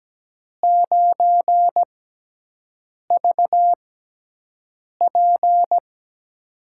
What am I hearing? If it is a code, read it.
Morse code, heard as a keyed tone, about 17 wpm, spaced as for 6 wpm, 700 Hz. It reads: 9VP
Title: USA Call Signs - 6x Character Spacing - 17wpm